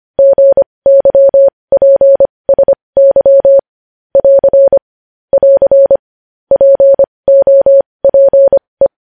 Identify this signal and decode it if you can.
Morse code, heard as a keyed tone, about 25 wpm, spaced as for 20 wpm, 550 Hz. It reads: GYPSY <AR> <AR> POPE
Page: news-headlines-in-morse-code-at-20-wpm